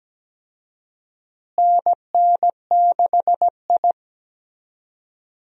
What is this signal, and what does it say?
Morse code, heard as a keyed tone, about 17 wpm, 700 Hz. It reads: NN6I
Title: USA Call Signs - 4 Characters - Repeated 3x - 17wpm